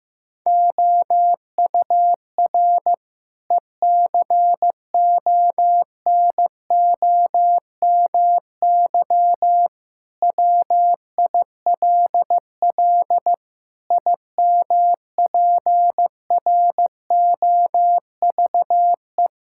Morse code, heard as a keyed tone, about 15 wpm, 700 Hz. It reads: OUR ECONOMY WILL IMPROVE
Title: Sentences from Top 1000 Words - Review Part 5 - 15wpm